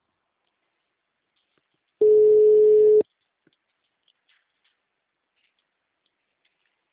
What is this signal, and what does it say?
Ringback tone (Europe, 425 Hz).